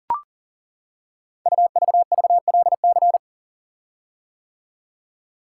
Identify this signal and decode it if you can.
Morse code, heard as a keyed tone, about 40 wpm, 700 Hz. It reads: UVVLC
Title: Code Groups - 5 Letters - 40wpm